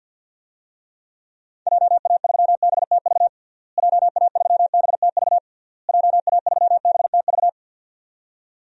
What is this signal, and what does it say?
Morse code, heard as a keyed tone, about 52 wpm, 700 Hz. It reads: JA3BTV JA3BTV JA3BTV
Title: DXCC-100 Call Signs - 6 Characters - Repeated 3x - 50wpm